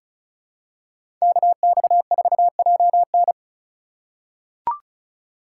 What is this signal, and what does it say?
Morse code, heard as a keyed tone, about 34 wpm, 700 Hz. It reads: KX4JN
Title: USA Call Signs - 4x Character Spacing - 35wpm